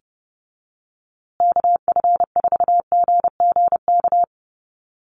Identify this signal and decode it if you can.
Morse code, heard as a keyed tone, about 30 wpm, 700 Hz. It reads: KF4GGK
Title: USA Call Signs Repeated 2x 30wpm